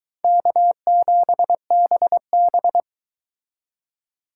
Morse code, heard as a keyed tone, about 22 wpm, 700 Hz. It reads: K7BB